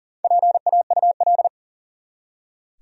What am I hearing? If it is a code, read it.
Morse code, heard as a keyed tone, about 37 wpm, 700 Hz. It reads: PAUL
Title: USA Common Names - Max 4 Letters - 40wpm